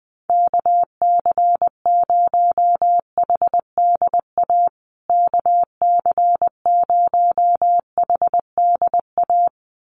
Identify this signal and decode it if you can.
Morse code, heard as a keyed tone, about 20 wpm, 700 Hz. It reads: KC0HDA KC0HDA
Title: USA Call Signs Repeated 2x 20wpm